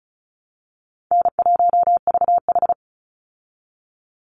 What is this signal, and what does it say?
Morse code, heard as a keyed tone, about 34 wpm, 700 Hz. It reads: N1VH
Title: Call Signs Repeated 2x - 35wpm